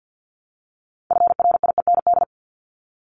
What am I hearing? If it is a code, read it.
Morse code, heard as a keyed tone, about 50 wpm, 700 Hz. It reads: FRIEND
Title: LY Suffix - Top 100 Words with Base Words 50wpm